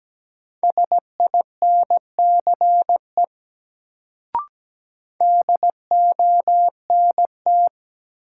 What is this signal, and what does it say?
Morse code, heard as a keyed tone, about 17 wpm, 700 Hz. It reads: SINCE DONT
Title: Sentences from Top 300 Words - New Word Drill - 17wpm